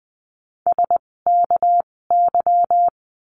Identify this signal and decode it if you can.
Morse code, heard as a keyed tone, about 20 wpm, spaced as for 16 wpm, 700 Hz. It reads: SKY
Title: 256 Lesson 42a - Words 12-20wpm